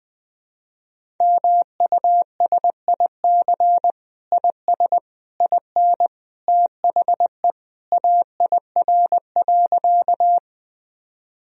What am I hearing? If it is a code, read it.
Morse code, heard as a keyed tone, about 20 wpm, 700 Hz. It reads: MUSIC IS IN THE AIR.
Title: Sentences from Top 1000 Words - Review Part 1 - 20wpm